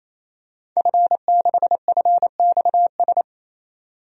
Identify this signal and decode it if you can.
Morse code, heard as a keyed tone, about 28 wpm, 690 Hz. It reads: F6FXS
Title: DXCC-100 Call Signs - 5 Characters - 28wpm